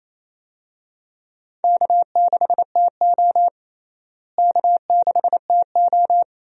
Morse code, heard as a keyed tone, about 28 wpm, 700 Hz. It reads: K6TO K6TO
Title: USA Call Signs - 4 Characters - Repeated 2x - 28wpm